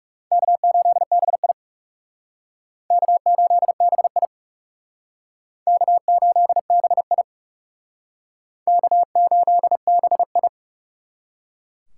Morse code, heard as a keyed tone, about 40 wpm, 700 Hz. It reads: K8BI K8BI K8BI K8BI